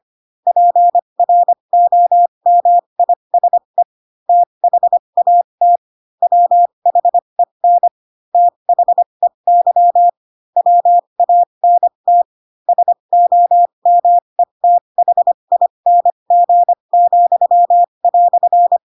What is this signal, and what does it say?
Morse code, heard as a keyed tone, about 25 wpm, spaced as for 22 wpm, 700 Hz. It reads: PROMISE THAT WHEN THEY WANT SOMETHING,"